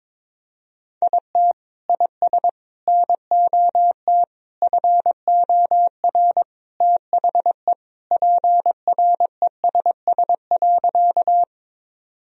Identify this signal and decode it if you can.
Morse code, heard as a keyed tone, about 22 wpm, 700 Hz. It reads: IT IS NOT FOR THE PRESS.
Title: Sentences from Top 1000 Words - Review Part 1 - 22wpm